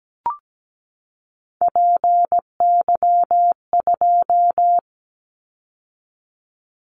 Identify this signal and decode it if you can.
Morse code, heard as a keyed tone, about 17 wpm, 700 Hz. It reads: PY2
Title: Callsign Prefixes for Top 100 DXCC - 17wpm